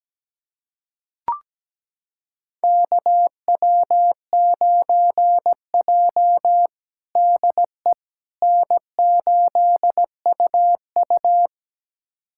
Morse code, heard as a keyed tone, about 17 wpm, 700 Hz. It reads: KW9J DE N8UU
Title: USA Callsign DE Callsign - 4 Characters - 17wpm